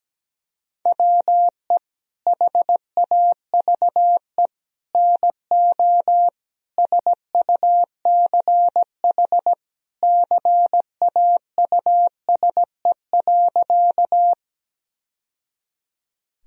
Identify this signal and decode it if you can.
Morse code, heard as a keyed tone, about 17 wpm, 700 Hz. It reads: WE HAVE NO SUCH CAUSE.
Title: Sentences from Top 1000 Words - Review Part 1 - 17wpm